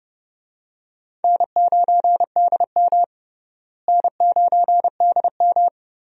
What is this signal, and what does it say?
Morse code, heard as a keyed tone, about 30 wpm, 700 Hz. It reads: N9DM N9DM
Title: USA Call Signs - 4 Characters - Repeated 2x - 30wpm